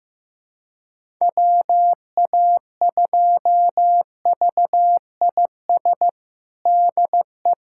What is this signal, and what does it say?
Morse code, heard as a keyed tone, about 15 wpm, 700 Hz. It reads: WA2VIS DE
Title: USA Callsign DE Callsign - 6 Characters - 15wpm